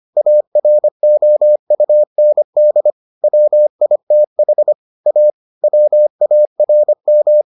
Morse code, heard as a keyed tone, about 25 wpm, 600 Hz. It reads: AROUND WITH A WARM